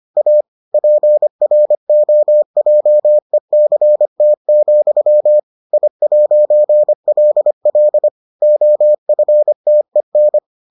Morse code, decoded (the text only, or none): A PROJECT, I'LL OFTEN